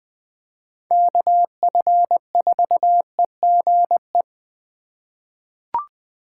Morse code, heard as a keyed tone, about 20 wpm, 700 Hz. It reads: KF4EGE
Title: USA Call Signs - 6 Characters - 4x Character Spacing - 20wpm